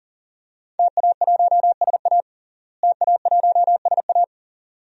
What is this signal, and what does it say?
Morse code, heard as a keyed tone, about 40 wpm, 700 Hz. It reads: TA1SA TA1SA
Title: DXCC-100 Call Signs - 5 Characters - Repeated 2x - 40wpm